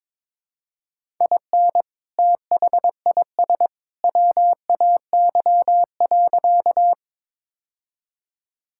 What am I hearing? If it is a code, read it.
Morse code, heard as a keyed tone, about 22 wpm, 700 Hz. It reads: IN THIS WAY.
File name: practice / 22wpm / Sentences with 2-Word Phrases - 22wpm